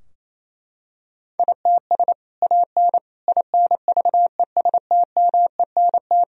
Morse code, heard as a keyed tone, about 27 wpm, 700 Hz. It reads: ITS AN INVESTMENT
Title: Sentences from Top 1000 Words - Review Part 4 - 28wpm